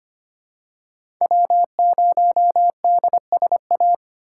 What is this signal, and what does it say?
Morse code, heard as a keyed tone, about 25 wpm, 700 Hz. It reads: W0DSA
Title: USA Call Signs - 5 Characters - 8x Character Spacing - 25wpm